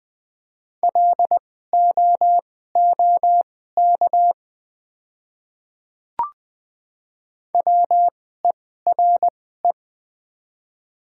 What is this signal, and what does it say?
Morse code, heard as a keyed tone, about 20 wpm, spaced as for 15 wpm, 700 Hz. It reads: LOOK WERE
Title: 2x Character Spacing - Top 200 Words - 20wpm